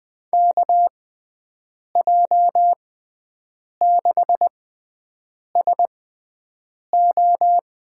Morse code, heard as a keyed tone, about 21 wpm, spaced as for 7 wpm, 700 Hz. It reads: KJ6SO
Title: USA Call Signs - 6 Characters - 6x Character Spacing - 20wpm